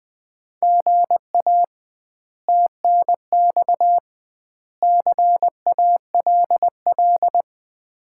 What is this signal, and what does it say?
Morse code, heard as a keyed tone, about 21 wpm, 700 Hz. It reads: GA TNX CALL
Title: Common QSO Phrases - 2x Word-Spacing - 20wpm